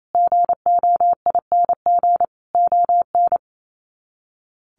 Morse code, heard as a keyed tone, about 28 wpm, 700 Hz. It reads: GOING ON
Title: Sets of 5 Words Encore2 28wpm